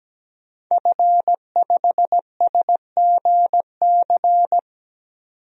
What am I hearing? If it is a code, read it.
Morse code, heard as a keyed tone, about 17 wpm, 700 Hz. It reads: F5SGC
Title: DXCC-100 Call Signs - 5 Characters - 17wpm